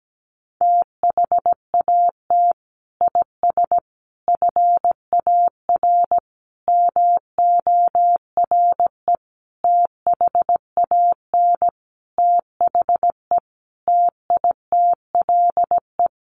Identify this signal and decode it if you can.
Morse code, heard as a keyed tone, about 17 wpm, 700 Hz. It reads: THAT IS FAR MORE THAN THE TITLE
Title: Sentences from Top 1200 Words 17wpm